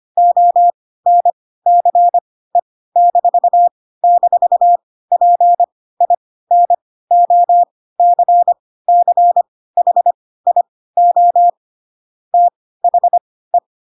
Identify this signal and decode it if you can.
Morse code, heard as a keyed tone, about 25 wpm, spaced as for 16 wpm, 700 Hz. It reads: ONCE--PINOCCHIO THE
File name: Pinocchio / Pinocchio ch3 10 25 ik1hge